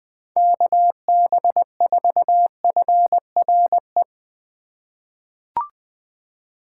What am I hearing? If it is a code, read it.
Morse code, heard as a keyed tone, about 21 wpm, 700 Hz. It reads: KB4FRE
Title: USA Call Signs - 6 Characters - 4x Character Spacing - 20wpm